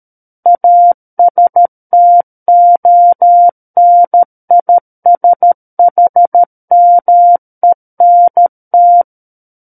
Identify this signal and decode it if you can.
Morse code, heard as a keyed tone, about 13 wpm, 700 Hz. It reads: ASTONISHMENT
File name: Book One / 04-Chapter 4 - 13 WPM